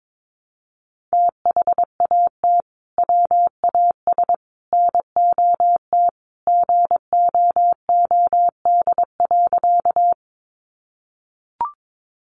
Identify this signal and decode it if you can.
Morse code, heard as a keyed tone, about 22 wpm, 700 Hz. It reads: THAT WAS NOT GOOD.
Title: Sentences from Top 600 Words 22wpm